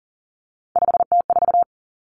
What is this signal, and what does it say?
Morse code, heard as a keyed tone, about 42 wpm, 700 Hz. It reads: 5T4
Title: Three Challenging Letter-Number Combinations 40wpm